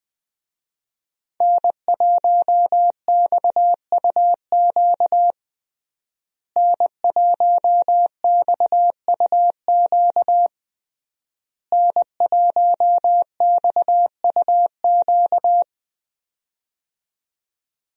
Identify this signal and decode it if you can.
Morse code, heard as a keyed tone, about 20 wpm, 700 Hz. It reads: N1XUQ N1XUQ N1XUQ
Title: USA Call Signs - 5 Characters - Repeated 3x - 20wpm